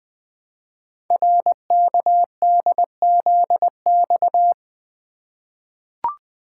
Morse code, heard as a keyed tone, about 21 wpm, 700 Hz. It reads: RKDZX
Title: Code Groups - 5 Letters - 20wpm